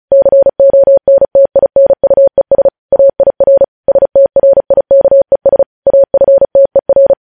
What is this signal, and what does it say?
Morse code, heard as a keyed tone, about 36 wpm, 550 Hz. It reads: CONTINUES AIR STRIKES AFTER
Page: News headlines in Morse code at 35 wpm